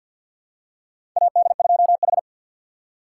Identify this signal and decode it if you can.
Morse code, heard as a keyed tone, about 48 wpm, 700 Hz. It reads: AD2H
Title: USA Call Signs - 4 Characters - 2x Character Spacing - 50wpm